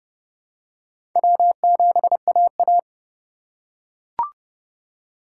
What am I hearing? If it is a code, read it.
Morse code, heard as a keyed tone, about 30 wpm, 700 Hz. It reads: W7AA
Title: USA Call Signs - 4 Characters - 3x Character Spacing - 30wpm